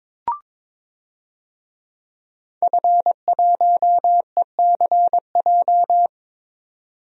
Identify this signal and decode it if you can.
Morse code, heard as a keyed tone, about 22 wpm, 700 Hz. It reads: F1ECJ
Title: DXCC-100 Call Signs - 5 Characters - 22wpm